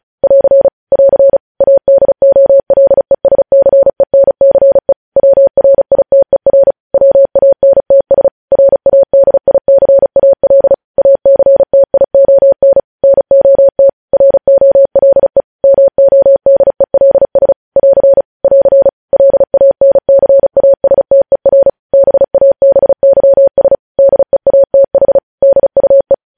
Morse code, heard as a keyed tone, about 36 wpm, 550 Hz. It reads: <AR> <AR> ADOLESCENCE WRITER WANTS RADICAL ACTION NOT ROLE MODELS <AR> <AR> LANCASTER BABYS DEATH DUE
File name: podcast / News headlines-1x-35x35